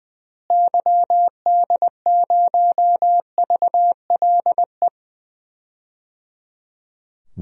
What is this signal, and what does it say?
Morse code, heard as a keyed tone, about 20 wpm, 700 Hz. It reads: YD0VLE
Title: DXCC-100 Call Signs - 6 Characters - Repeated 3x - 20wpm